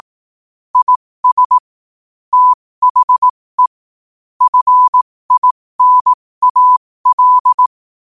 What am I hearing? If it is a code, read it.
Morse code, heard as a keyed tone, about 18 wpm, spaced as for 15 wpm, 1000 Hz. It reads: IS THE FINAL
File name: qotdfiles / 15wpm